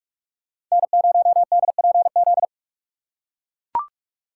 Morse code, heard as a keyed tone, about 44 wpm, 700 Hz. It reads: N0DPZ